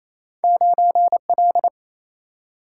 Morse code, heard as a keyed tone, about 28 wpm, 700 Hz. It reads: 9L
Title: Single-Letter Single-Number 28wpm